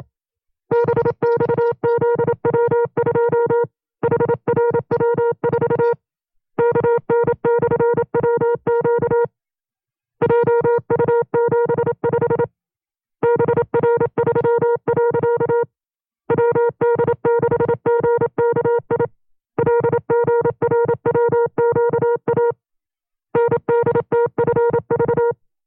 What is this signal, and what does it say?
Morse code, heard as a keyed tone, about 27 wpm, 470 Hz. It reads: BXZW2 HRW4 KN/WQ JU75 BR3. WD6GKI LGRWQA NDTFV
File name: 25WPM